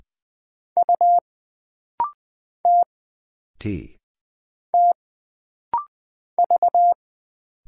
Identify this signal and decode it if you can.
Morse code, heard as a keyed tone, about 20 wpm, 700 Hz. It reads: U T T V
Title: Single Letter-Number - Rapid-Fire 20wpm